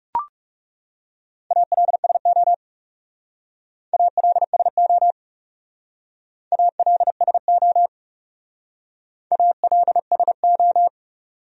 Morse code, heard as a keyed tone, about 36 wpm, 700 Hz. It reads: ALSO ALSO ALSO ALSO
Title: Speed-Racing Top 300 Words 30wpm